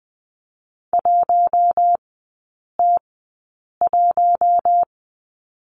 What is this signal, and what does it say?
Morse code, heard as a keyed tone, about 20 wpm, spaced as for 8 wpm, 700 Hz. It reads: 1T1
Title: 041 Lesson 8e - Number 1 - Three Characters 20wpm